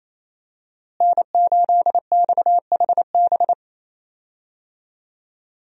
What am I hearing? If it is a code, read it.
Morse code, heard as a keyed tone, about 28 wpm, 700 Hz. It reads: N8XHB
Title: USA Call Signs - 5 Characters - Repeated 2x - 28wpm